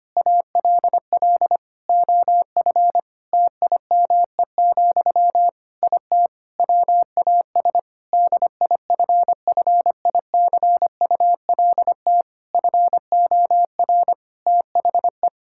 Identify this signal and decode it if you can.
Morse code, heard as a keyed tone, about 25 wpm, 700 Hz. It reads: ALL OF TIME, IT WAS DIFFICULT FOR THE
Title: Sentences from Top 1000 Words - Review Part 5 - 25wpm